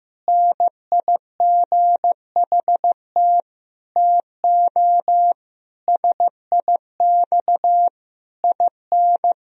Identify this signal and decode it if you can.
Morse code, heard as a keyed tone, about 16 wpm, 700 Hz. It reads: NIGHT TO SIX IN
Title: Sentences from Top 1000 Words - Review Part 2 - 15wpm